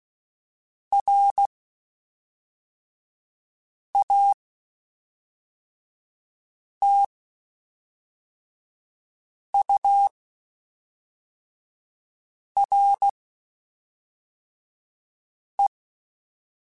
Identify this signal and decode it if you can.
Morse code, heard as a keyed tone, about 16 wpm, 800 Hz. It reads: R A T U R E
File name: Spaced Characters / Spaced Characters-mp3s / Weather 4 10 sp @5